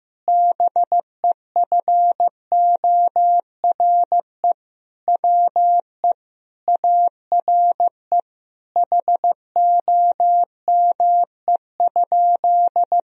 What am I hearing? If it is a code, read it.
Morse code, heard as a keyed tone, about 15 wpm, 700 Hz. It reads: BEFORE WE ARE HOME?
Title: Sentences from Top 1000 Words - Review Part 1 - 15wpm